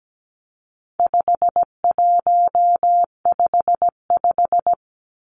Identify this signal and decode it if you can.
Morse code, heard as a keyed tone, about 17 wpm, 700 Hz. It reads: 5155
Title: Numbers - 4-Digit 17wpm